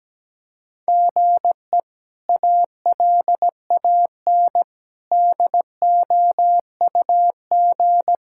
Morse code, heard as a keyed tone, about 17 wpm, 700 Hz. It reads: GE ALAN DOUG
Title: K1USN SST QSO Practice - 17wpm